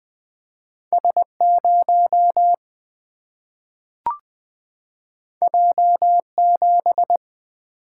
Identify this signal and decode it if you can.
Morse code, heard as a keyed tone, about 20 wpm, 700 Hz. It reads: S0 J7
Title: Single-Letter Single-Number 20wpm